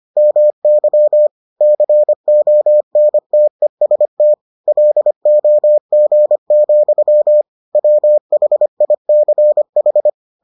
Morse code, heard as a keyed tone, about 25 wpm, 600 Hz. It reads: MY CONTEST LOG, WHICH